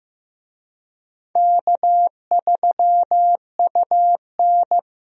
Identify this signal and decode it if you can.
Morse code, heard as a keyed tone, about 15 wpm, 700 Hz. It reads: K3UN
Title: USA Call Signs - 4 Characters - Repeated 2x - 15wpm